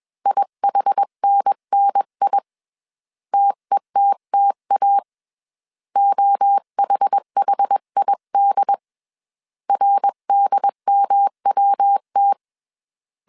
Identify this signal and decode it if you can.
Morse code, heard as a keyed tone, about 21 wpm, 800 Hz. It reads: IHNNI TETTA OHHID RDMWT
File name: CW / DET / 1 3 1500C S B 20wpm 1754